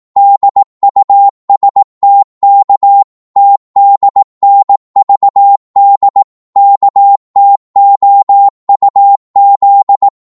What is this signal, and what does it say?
Morse code, heard as a keyed tone, about 18 wpm, 800 Hz. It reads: DUSTK TDNVD KTOUZ
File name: Morse / ltr-18wpm